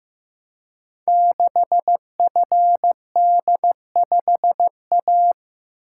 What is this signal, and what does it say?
Morse code, heard as a keyed tone, about 15 wpm, 700 Hz. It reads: 6FD5A
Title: Code Groups - 5 Letter:Numbers - 15wpm